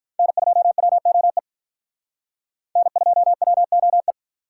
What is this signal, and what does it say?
Morse code, heard as a keyed tone, about 44 wpm, 700 Hz. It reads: N2WOE N2WOE